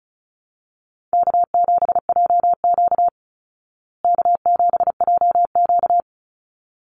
Morse code, heard as a keyed tone, about 34 wpm, 700 Hz. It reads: K7JQ K7JQ
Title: Call Signs Repeated 3x - 35wpm